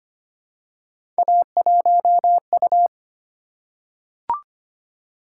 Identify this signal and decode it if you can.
Morse code, heard as a keyed tone, about 25 wpm, 700 Hz. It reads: A1U
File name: Three Letter-Number Combinations - 25wpm